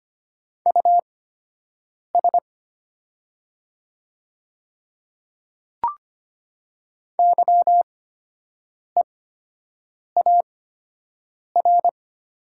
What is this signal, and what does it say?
Morse code, heard as a keyed tone, about 25 wpm, spaced as for 7 wpm, 700 Hz. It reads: US YEAR
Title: 8x Character Spacing - Top 200 Words - 25wpm